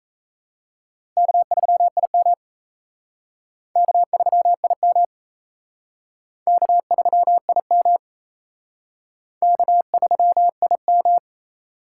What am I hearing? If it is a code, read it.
Morse code, heard as a keyed tone, about 34 wpm, 700 Hz. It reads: K3IM K3IM K3IM K3IM
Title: Speed Racing - USA 1x2 Call Signs 28wpm